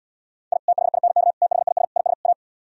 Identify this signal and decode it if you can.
Morse code, heard as a keyed tone, about 78 wpm, 700 Hz. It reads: I THINK THIS IS A